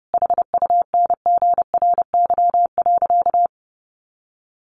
Morse code, heard as a keyed tone, about 30 wpm, 700 Hz. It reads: HUNGRY.
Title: Sentences from Top 1400 Words 30wpm